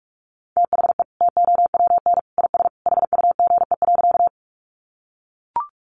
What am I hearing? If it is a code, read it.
Morse code, heard as a keyed tone, about 44 wpm, 700 Hz. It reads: THE TOWN IS HUGE.
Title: Sentences from Top 1100 Words 45wpm